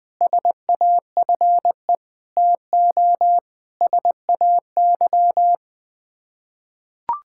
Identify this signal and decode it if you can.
Morse code, heard as a keyed tone, about 21 wpm, 700 Hz. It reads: SAFE TO SAY
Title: Sets of 6 Words - 20wpm